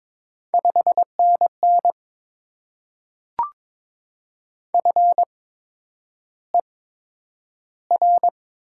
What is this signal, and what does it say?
Morse code, heard as a keyed tone, about 22 wpm, spaced as for 9 wpm, 700 Hz. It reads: 5NN FER
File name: Top 20 QSO Elements - 8x Character Spacing - 22wpm